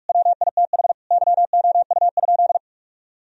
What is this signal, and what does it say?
Morse code, heard as a keyed tone, about 46 wpm, 700 Hz. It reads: WITH YOU?
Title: Sentences from Top 1000 Words - Review Part 2 - 45wpm